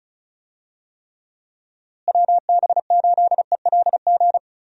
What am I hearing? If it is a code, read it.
Morse code, heard as a keyed tone, about 36 wpm, 700 Hz. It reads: WB8ELG
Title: USA Call Signs - 6 Characters - Repeated 2x - 35wpm